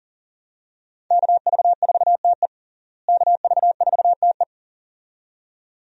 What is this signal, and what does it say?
Morse code, heard as a keyed tone, about 40 wpm, 700 Hz. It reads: KV4TE KV4TE
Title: USA Call Signs - 5 Characters - Repeated 2x - 40wpm